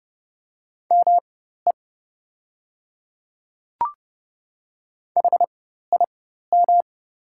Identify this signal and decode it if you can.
Morse code, heard as a keyed tone, about 30 wpm, spaced as for 14 wpm, 700 Hz. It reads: ME HIM
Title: 4x Character Spacing - Top 200 Words - 30wpm